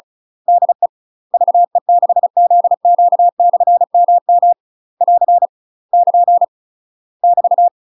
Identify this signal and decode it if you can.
Morse code, heard as a keyed tone, about 35 wpm, 700 Hz. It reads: DE VE6ZQ/MM <AR> <KN> <BT>
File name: UK MASTER QSOs TRAINING/35WPM